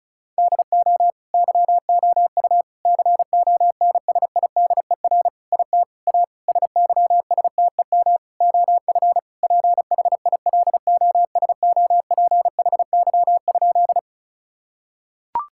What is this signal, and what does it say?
Morse code, heard as a keyed tone, about 34 wpm, 700 Hz. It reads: DO YOU CONSIDER IT A SYSTEM OF PHILOSOPHY?
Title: Sentences from Top 1000 Words - Review Part 6 - 35wpm